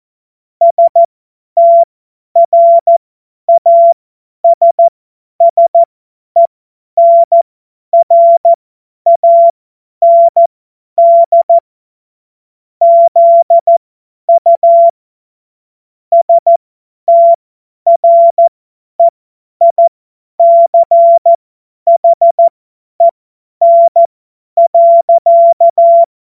Morse code, heard as a keyed tone, about 14 wpm, spaced as for 10 wpm, 680 Hz. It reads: STRASSENRAND ZU STREICHEN.
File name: Morsekurs / Witze / Witz 30. Dezember 2025 50 F70